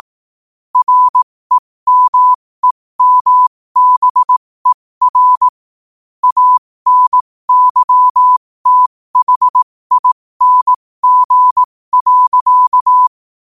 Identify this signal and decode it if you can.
Morse code, heard as a keyed tone, about 18 wpm, spaced as for 15 wpm, 1000 Hz. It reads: REMEMBER ANYTHING.
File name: qotdfiles / 15wpm